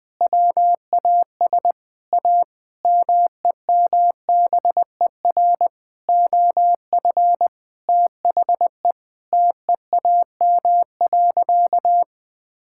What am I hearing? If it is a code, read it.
Morse code, heard as a keyed tone, about 20 wpm, 700 Hz. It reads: WAS A MEMBER OF THE TEAM.
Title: Sentences with 5-Word Phrases - 20wpm